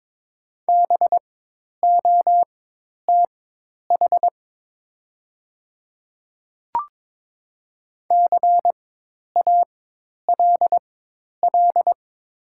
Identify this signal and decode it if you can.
Morse code, heard as a keyed tone, about 22 wpm, spaced as for 10 wpm, 700 Hz. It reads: BOTH CALL
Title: 4x Character Spacing - Top 300 Words - 22wpm